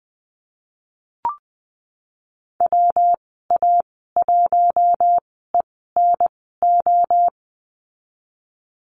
Morse code, heard as a keyed tone, about 20 wpm, spaced as for 15 wpm, 700 Hz. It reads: WA1ENO
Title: 179 Lesson 29g - Slash - Call Signs 20wpm